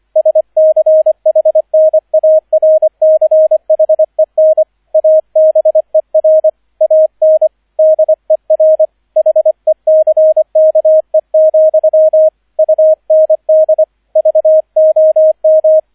Morse code, heard as a keyed tone, about 25 wpm, 630 Hz, with radio noise. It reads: SCHNARCHEN ABER AN DER HECKE, UND VOM